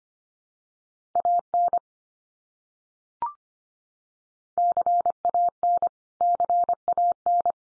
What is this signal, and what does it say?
Morse code, heard as a keyed tone, about 25 wpm, 700 Hz. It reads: AN CAN CAN
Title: Top 100 Words 25wpm x3